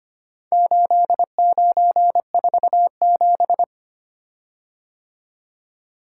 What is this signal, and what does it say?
Morse code, heard as a keyed tone, about 25 wpm, 690 Hz. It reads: 8947
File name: practice / 25wpm / Four Letter-Number Combinations - 25wpm